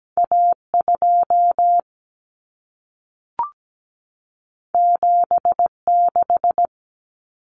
Morse code, heard as a keyed tone, about 17 wpm, 700 Hz. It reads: A2 76
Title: Common Cut Numbers - 2-Digit 17wpm